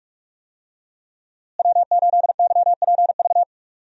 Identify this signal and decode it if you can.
Morse code, heard as a keyed tone, about 46 wpm, 700 Hz. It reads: W8YPV
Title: USA Call Signs - 5 Characters - Repeated 2x - 45wpm